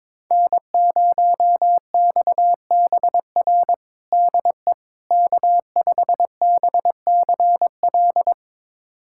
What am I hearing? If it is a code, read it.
Morse code, heard as a keyed tone, about 22 wpm, 700 Hz. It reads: N0XBR DE K5BCL
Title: USA Callsign DE Callsign - 5 Characters - 22wpm